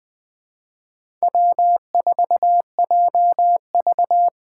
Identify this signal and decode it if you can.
Morse code, heard as a keyed tone, about 20 wpm, 700 Hz. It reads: W4JV
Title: USA Call Signs - 4x Character Spacing - 20wpm